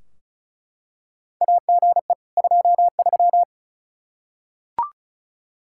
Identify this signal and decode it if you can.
Morse code, heard as a keyed tone, about 36 wpm, 700 Hz. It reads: AGE 23
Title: Practice QSO - Exchange 3 - More Info - 35wpm